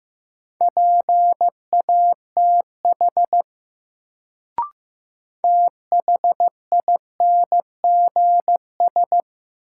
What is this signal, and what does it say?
Morse code, heard as a keyed tone, about 15 wpm, 700 Hz. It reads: PATH THINGS
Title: Sentences from Top 1400 Words - New Word Drill Alt - 15wpm